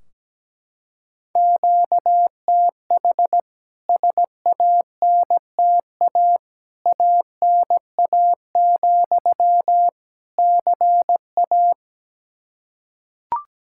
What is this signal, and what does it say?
Morse code, heard as a keyed tone, about 17 wpm, 700 Hz. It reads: QTH SANTA ANA, CA
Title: K1USN SST QSO Practice 17wpm